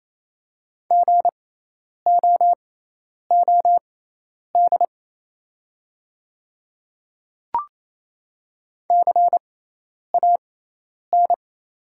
Morse code, heard as a keyed tone, about 28 wpm, spaced as for 10 wpm, 690 Hz. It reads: GOOD CAN
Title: 6x Character Spacing - Top 300 Words - 28wpm